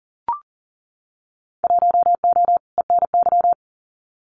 Morse code, heard as a keyed tone, about 40 wpm, 700 Hz. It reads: 1O ENY
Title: Winter Field Day Exchanges 40wpm